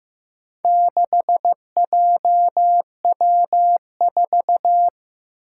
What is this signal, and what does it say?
Morse code, heard as a keyed tone, about 15 wpm, 700 Hz. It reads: 6JW4
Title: Four Challenging Letter-Number Combinations 15wpm